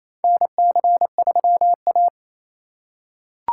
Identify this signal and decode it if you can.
Morse code, heard as a keyed tone, about 28 wpm, 700 Hz. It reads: NC3A
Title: USA Call Signs - 4 Characters - 2x Character Spacing - 28wpm